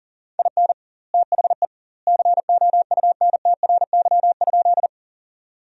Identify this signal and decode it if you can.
Morse code, heard as a keyed tone, about 40 wpm, 700 Hz. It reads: IN THE COUNTRY?
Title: 2x Word-Spacing - Sentences from Top 300 Words - 40wpm